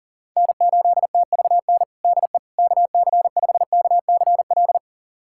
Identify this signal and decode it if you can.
Morse code, heard as a keyed tone, about 40 wpm, 700 Hz. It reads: N8TVN DE KC5KCL
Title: USA Callsign DE Callsign 40wpm